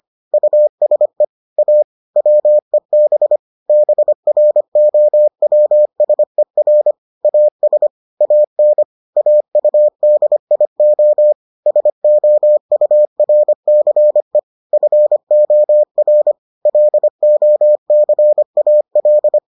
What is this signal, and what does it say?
Morse code, heard as a keyed tone, about 25 wpm, 600 Hz. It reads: USE A WEB BROWSER AS AN AUDIO SOURCE FOR LOCAL